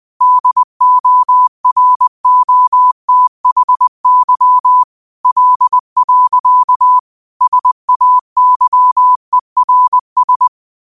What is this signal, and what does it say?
Morse code, heard as a keyed tone, about 20 wpm, 1000 Hz. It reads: DOROTHY L. SAYERS
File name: 20wpm